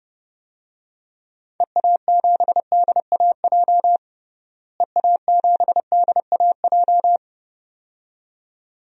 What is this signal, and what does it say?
Morse code, heard as a keyed tone, about 30 wpm, 700 Hz. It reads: EA7DAJ EA7DAJ
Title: DXCC-100 Call Signs - 6 Characters - Repeated 2x - 30wpm